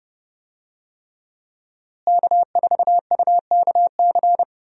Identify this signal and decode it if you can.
Morse code, heard as a keyed tone, about 28 wpm, 700 Hz. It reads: K4UKC
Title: USA Call Signs - 5 Characters - Repeated 3x - 30wpm